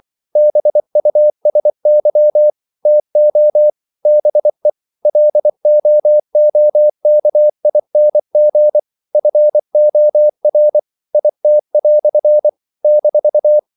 Morse code, heard as a keyed tone, about 24 wpm, 600 Hz. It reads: BUSY TO BE LOOKING FOR IT" -
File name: CW / Practice / EnglishQuotations0224